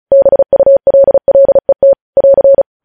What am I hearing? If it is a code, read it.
Morse code, heard as a keyed tone, about 36 wpm, 550 Hz. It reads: BULLET <AR>
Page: news headlines in morse code at 35 wpm